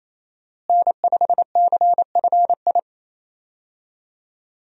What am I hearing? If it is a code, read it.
Morse code, heard as a keyed tone, about 28 wpm, 700 Hz. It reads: N5CFI
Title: USA Call Signs - 5 Characters - Repeated 3x - 28wpm